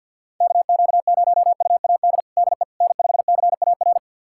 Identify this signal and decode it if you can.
Morse code, heard as a keyed tone, about 52 wpm, 700 Hz. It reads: KQ0UAD DE N5CAR
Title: USA Callsign DE Callsign 50wpm